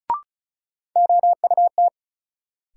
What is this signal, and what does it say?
Morse code, heard as a keyed tone, about 34 wpm, 700 Hz. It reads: OUT
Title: Sentences from Top 100 Words - New Word Drill - 35wpm